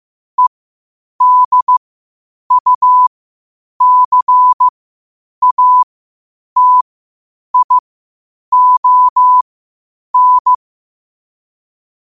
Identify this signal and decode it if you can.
Morse code, heard as a keyed tone, about 15 wpm, spaced as for 8 wpm, 1000 Hz. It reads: EDUCATION